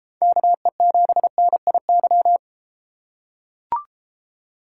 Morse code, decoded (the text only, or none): KE7NIY